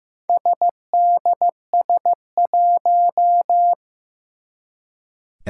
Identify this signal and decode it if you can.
Morse code, heard as a keyed tone, about 15 wpm, 700 Hz. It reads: SDS1
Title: Four Challenging Letter-Number Combinations 15wpm